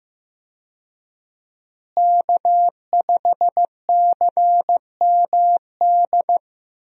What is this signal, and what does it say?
Morse code, heard as a keyed tone, about 15 wpm, 700 Hz. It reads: K5CMD
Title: USA Call Signs - 5 Characters - Repeated 2x - 15wpm